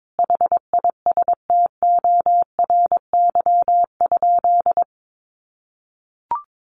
Morse code, heard as a keyed tone, about 22 wpm, 700 Hz. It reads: HISTORY?
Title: Sentences from Top 1500 Words 22wpm